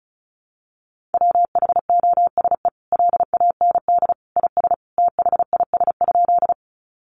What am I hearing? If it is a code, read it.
Morse code, heard as a keyed tone, about 34 wpm, 700 Hz. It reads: WHOSE LAND IS THIS?
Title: Sentences from Top 1200 Words 35wpm